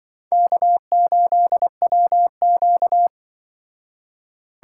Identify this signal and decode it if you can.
Morse code, heard as a keyed tone, about 24 wpm, 700 Hz. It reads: K8WQ
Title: Speed Racing - USA Call Signs 20wpm